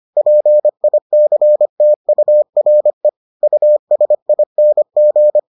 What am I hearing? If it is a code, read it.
Morse code, heard as a keyed tone, about 25 wpm, 600 Hz. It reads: PICTURE USING